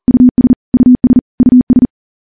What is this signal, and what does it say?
Morse code, heard as a keyed tone, about 40 wpm, 250 Hz. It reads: US US US
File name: morse-training-40WPM